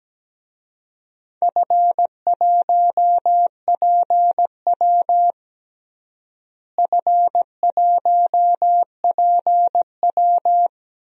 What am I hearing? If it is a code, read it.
Morse code, heard as a keyed tone, about 17 wpm, 700 Hz. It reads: F1PW F1PW
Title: DXCC-100 Call Signs - 4 Characters - Repeated 2x - 17wpm